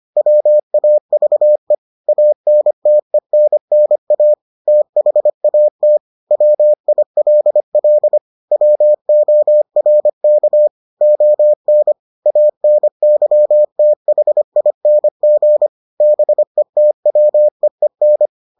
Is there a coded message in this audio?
Yes: Morse code from 0.2 to 18.3 s, heard as a keyed tone, about 25 wpm, 600 Hz. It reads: WAVE ANTENNA THAT WILL WORK ON ANYTHING BETWEEN